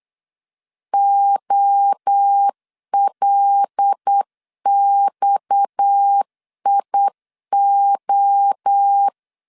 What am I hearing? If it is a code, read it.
Morse code, heard as a keyed tone, about 8 wpm, 800 Hz. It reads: OLXIO
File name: LEZIONI / 8 YJ45 / 8 YJ 8wpm 3313